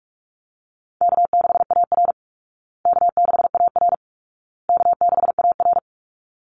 Morse code, heard as a keyed tone, about 48 wpm, 700 Hz. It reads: K6AR K6AR K6AR
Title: Easy Call Signs Repeated 3x - 45wpm